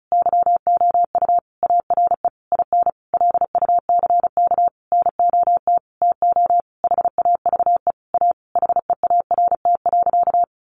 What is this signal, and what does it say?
Morse code, heard as a keyed tone, about 34 wpm, 700 Hz. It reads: YOU ARE IN LUCK NOT TO HAVE A HEART.
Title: Sentences from Top 1500 Words 35wpm